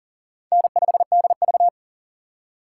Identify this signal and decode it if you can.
Morse code, heard as a keyed tone, about 52 wpm, 700 Hz. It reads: N5DV